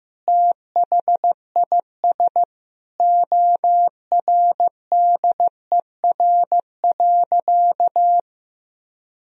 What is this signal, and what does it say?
Morse code, heard as a keyed tone, about 15 wpm, 700 Hz. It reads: THIS ORDER.
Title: Sentences from Top 1000 Words - Review Part 2 - 15wpm